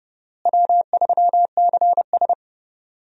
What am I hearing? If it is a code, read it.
Morse code, heard as a keyed tone, about 30 wpm, 700 Hz. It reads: W3CS
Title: USA Call Signs - 4 Characters - Repeated 2x - 30wpm